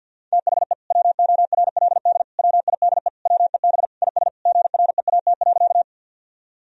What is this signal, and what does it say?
Morse code, heard as a keyed tone, about 51 wpm, 700 Hz. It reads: THE WORLD WIDE WEB IS GREAT.
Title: Sentences from Top 1000 Words - Review Part 4 - 50wpm